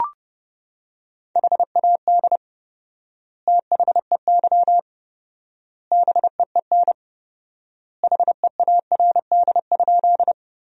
Morse code, heard as a keyed tone, about 30 wpm, 700 Hz. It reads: HAD THEY BEEN HEARD?
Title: 4x Word-Spacing - Sentences from Top 400 Words - 30wpm